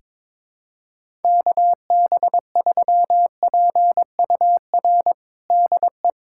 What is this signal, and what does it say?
Morse code, heard as a keyed tone, about 22 wpm, 700 Hz. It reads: KB3PUR DE
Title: USA Callsign DE Callsign - 6 Characters - 22wpm